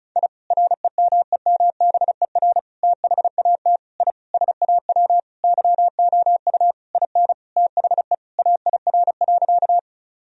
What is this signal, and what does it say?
Morse code, heard as a keyed tone, about 36 wpm, 700 Hz. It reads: I REMEMBER THAT I SAW YOU IN THE AIR.
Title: Sentences from Top 1000 Words - Review Part 1 - 35wpm